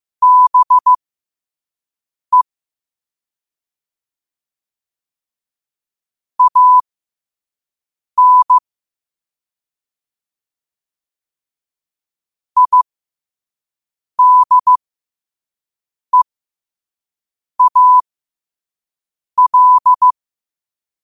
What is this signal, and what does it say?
Morse code, heard as a keyed tone, about 15 wpm, spaced as for 5 wpm, 1000 Hz. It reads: BE AN IDEAL
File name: qotdfiles / 5wpm